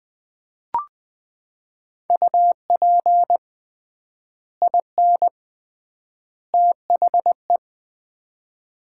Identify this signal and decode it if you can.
Morse code, heard as a keyed tone, about 20 wpm, 700 Hz. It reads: UP IN THE
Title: 3x Word-Spacing - Sets of 3 Words - 20wpm